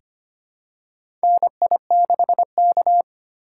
Morse code, heard as a keyed tone, about 25 wpm, 700 Hz. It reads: NI6K
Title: USA Call Signs - 4x Character Spacing - 25wpm